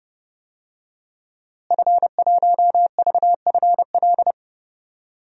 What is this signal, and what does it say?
Morse code, heard as a keyed tone, about 31 wpm, 700 Hz. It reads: F1VFL
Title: DXCC-100 Call Signs - 5 Characters - Repeated 3x - 30wpm